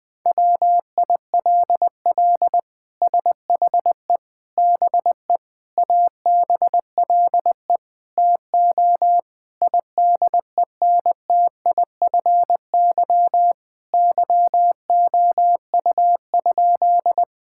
Morse code, heard as a keyed tone, about 20 wpm, 700 Hz. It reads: WILL SHE BE ABLE TO IDENTIFY YOU?
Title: Sentences from Top 1000 Words - Review Part 6 - 20wpm